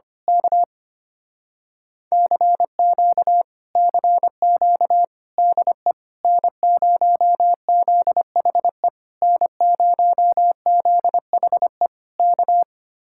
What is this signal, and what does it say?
Morse code, heard as a keyed tone, about 24 wpm, 700 Hz. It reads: K CQ CQ DE N0ZHE N0ZHE K
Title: Speed-Racing Common QSO Phrases Encore - 25wpm